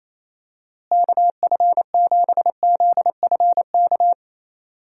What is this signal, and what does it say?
Morse code, heard as a keyed tone, about 28 wpm, 700 Hz. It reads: KF7ZFK
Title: USA Call Signs - 6 Characters - Repeated 3x - 28wpm